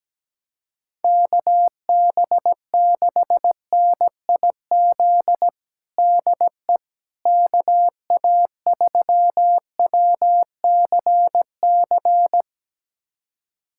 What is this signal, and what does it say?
Morse code, heard as a keyed tone, about 17 wpm, 700 Hz. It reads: KB6NIZ DE KA3WCC
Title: USA Callsign DE Callsign - 6 Characters - 17wpm